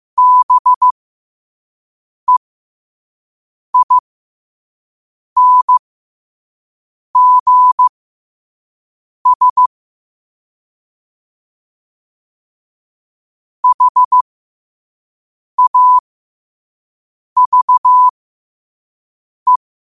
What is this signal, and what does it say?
Morse code, heard as a keyed tone, about 15 wpm, spaced as for 5 wpm, 1000 Hz. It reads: BEINGS HAVE
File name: qotdfiles / 5wpm